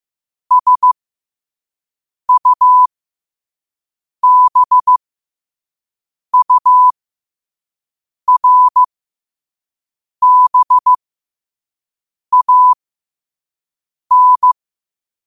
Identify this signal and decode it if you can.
Morse code, heard as a keyed tone, about 15 wpm, spaced as for 5 wpm, 1000 Hz. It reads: SUBURBAN
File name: 2024-10-07-5wpm